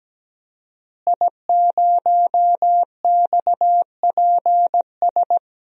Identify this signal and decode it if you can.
Morse code, heard as a keyed tone, about 17 wpm, 700 Hz. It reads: I0XPS
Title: DXCC-100 Call Signs - 5 Characters - Repeated 2x - 17wpm